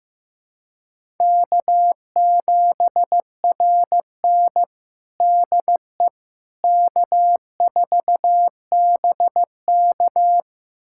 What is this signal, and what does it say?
Morse code, heard as a keyed tone, about 15 wpm, 700 Hz. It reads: K7RN DE K4BK
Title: USA Callsign DE Callsign - 4 Characters - 15wpm